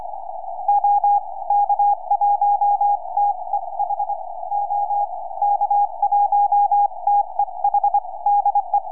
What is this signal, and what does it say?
Morse code, heard as a keyed tone, about 24 wpm, spaced as for 20 wpm, 820 Hz. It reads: OK1TEH OK1TEH DE
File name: ec7amy 4m